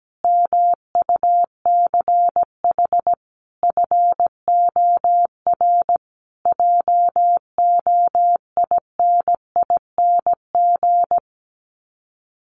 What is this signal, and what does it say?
Morse code, heard as a keyed tone, about 17 wpm, 700 Hz. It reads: MUCH FOR JOINING
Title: Sets of 5 Words Encore 17wpm